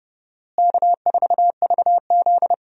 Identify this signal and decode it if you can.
Morse code, heard as a keyed tone, about 30 wpm, 700 Hz. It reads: K4VZ
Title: USA Call Signs - 4 Characters - Repeated 3x - 30wpm